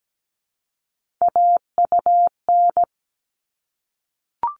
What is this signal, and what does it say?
Morse code, heard as a keyed tone, about 17 wpm, 700 Hz. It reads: AUN
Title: All Cut Numbers - 3-Digit 17wpm